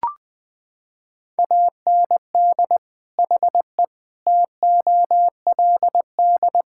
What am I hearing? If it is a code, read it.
Morse code, heard as a keyed tone, about 20 wpm, 700 Hz. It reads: AND HE TOLD
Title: Sentences from Top 1000 Words - Review Part 5 - 20wpm